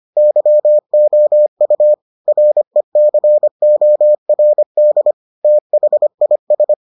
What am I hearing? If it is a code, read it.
Morse code, heard as a keyed tone, about 25 wpm, 600 Hz. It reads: YOU RECORD THIS